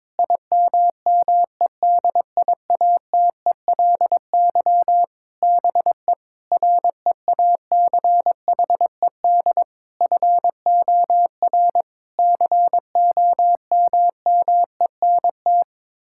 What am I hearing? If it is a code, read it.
Morse code, heard as a keyed tone, about 22 wpm, 700 Hz. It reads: IMMEDIATELY BE REACHED FOR COMMENT
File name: practice / 22wpm / Sets of 7 Words - 22wpm